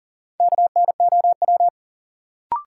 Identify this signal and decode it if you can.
Morse code, heard as a keyed tone, about 40 wpm, 700 Hz. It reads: KNOW
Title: Sentences from Top 100 Words - New Word Drill - 40wpm